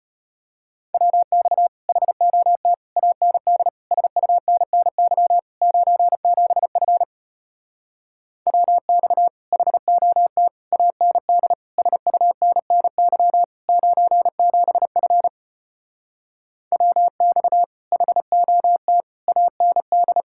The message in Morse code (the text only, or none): WX HOT AND SUNNY 97F WX HOT AND SUNNY 97F WX HOT AND